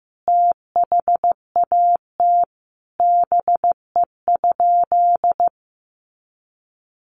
Morse code, heard as a keyed tone, about 15 wpm, 700 Hz. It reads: THAT BE?
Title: Sentences from Top 100 Words 15wpm